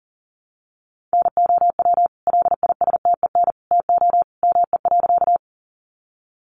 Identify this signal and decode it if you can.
Morse code, heard as a keyed tone, about 40 wpm, 700 Hz. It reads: NOW LISTEN TO ME.
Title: Sentences from Top 400 Words 40wpm